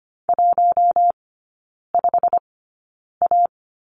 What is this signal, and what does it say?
Morse code, heard as a keyed tone, about 25 wpm, spaced as for 9 wpm, 700 Hz. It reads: 15A
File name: course / 25:10wpm / 093 Lesson 16e - Letter c - Three Characters 25wpm